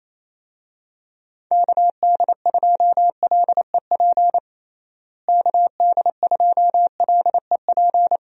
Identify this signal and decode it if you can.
Morse code, heard as a keyed tone, about 28 wpm, 700 Hz. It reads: KD2LEP KD2LEP
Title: USA Call Signs - 6 Characters - Repeated 3x - 28wpm